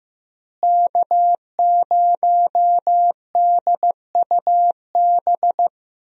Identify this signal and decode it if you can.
Morse code, heard as a keyed tone, about 15 wpm, 700 Hz. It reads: K0DUB
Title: USA Call Signs - 5 Characters - Repeated 2x - 15wpm